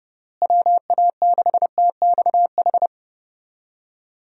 Morse code, heard as a keyed tone, about 30 wpm, 700 Hz. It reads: WA6TXH
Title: USA Call Signs - 6 Characters - Repeated 2x - 30wpm